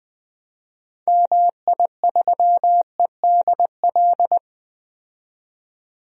Morse code, heard as a keyed tone, about 20 wpm, 700 Hz. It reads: MI3EDL
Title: DXCC-100 Call Signs - 6 Characters - 20wpm